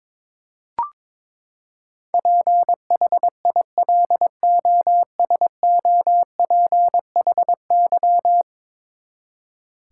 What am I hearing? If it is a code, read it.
Morse code, heard as a keyed tone, about 22 wpm, 700 Hz. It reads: PHILOSOPHY
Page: morse-code-ninja-practice-22wpm